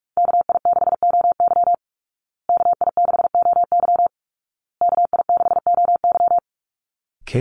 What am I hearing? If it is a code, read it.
Morse code, heard as a keyed tone, about 44 wpm, 700 Hz. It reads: KI6OY KI6OY KI6OY
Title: Call Signs Repeated 2x - 45wpm